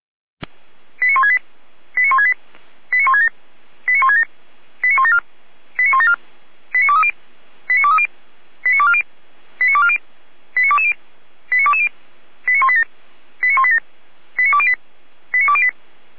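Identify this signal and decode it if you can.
Five-tone selective call (ZVEI), heard as quick runs of five tones, tones of about 70 ms. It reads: 89168 89168 89167 89167 89174 89174 89230 89230 89240 89240 89220 89220 89187 89187 89208 89208
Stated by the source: ZVEI five-tone call 89150 received on an SDR.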